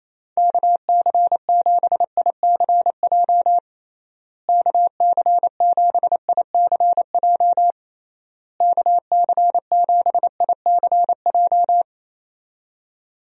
Morse code, heard as a keyed tone, about 28 wpm, 700 Hz. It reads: KC7ICJ KC7ICJ KC7ICJ
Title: USA Call Signs - 6 Characters - Repeated 3x - 28wpm